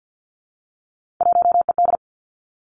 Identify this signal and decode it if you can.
Morse code, heard as a keyed tone, about 50 wpm, 700 Hz. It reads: 1ED
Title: Three Challenging Letter-Number Combinations 50wpm